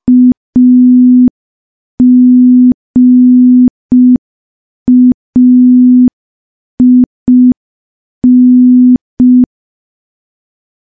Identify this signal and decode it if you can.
Morse code, heard as a keyed tone, about 5 wpm, 250 Hz. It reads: AGAIN
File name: morse / morse-training-05WPM